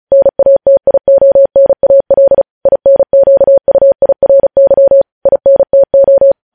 Morse code, heard as a keyed tone, about 36 wpm, 550 Hz. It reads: NATIONAL INQUIRY INTO
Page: news headlines in morse code at 35 wpm